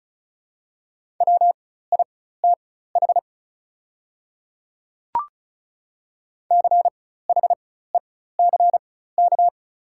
Morse code, heard as a keyed tone, about 34 wpm, spaced as for 16 wpm, 700 Hz. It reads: WITH CHECK
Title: 4x Character Spacing - Top 500 Words - 35wpm